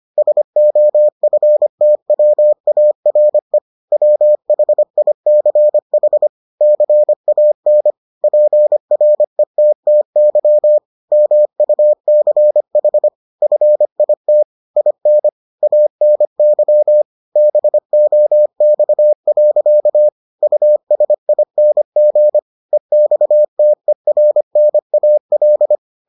Morse code, heard as a keyed tone, about 25 wpm, 600 Hz. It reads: SOFTWARE WHICH CAN PRETTY MUCH FIT IN ANY BOX. USING EXTERNAL